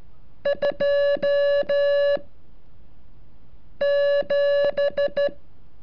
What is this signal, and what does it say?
Morse code, heard as a keyed tone, about 13 wpm, 570 Hz. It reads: 2 7
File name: curso / ejercicio 35